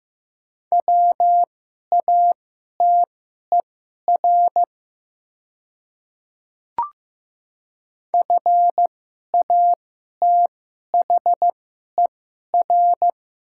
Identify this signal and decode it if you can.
Morse code, heard as a keyed tone, about 16 wpm, spaced as for 11 wpm, 700 Hz. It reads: WATER FATHER
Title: 2x Character Spacing - Top 500 Words - 15wpm